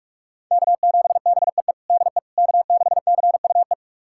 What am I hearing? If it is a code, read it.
Morse code, heard as a keyed tone, about 46 wpm, 700 Hz. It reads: K7BEE DE K6CUE
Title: USA Callsign DE Callsign - 5 Characters - 45wpm